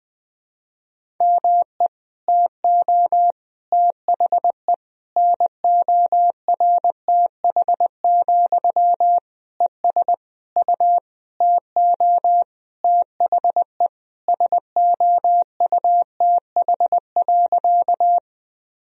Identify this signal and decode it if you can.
Morse code, heard as a keyed tone, about 20 wpm, 700 Hz. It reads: ME TO THE NORTH, ES U TO THE SOUTH.
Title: Sentences from Top 400 Words and Abbreviations 20wpm